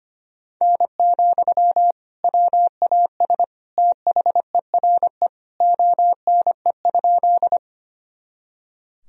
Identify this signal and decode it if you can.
Morse code, heard as a keyed tone, about 25 wpm, 700 Hz. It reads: N, WAS THERE ONE?
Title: Sentences from Top 100 Words and Abbreviations 25wpm